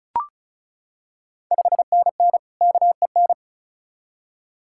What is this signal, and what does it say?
Morse code, heard as a keyed tone, about 36 wpm, 690 Hz. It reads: 5NN KEN
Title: Ontario QSO Party Exchange 35wpm